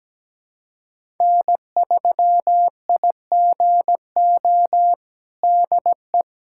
Morse code, heard as a keyed tone, about 17 wpm, 700 Hz. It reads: N3IGO DE
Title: USA Callsign DE Callsign - 5 Characters - 17wpm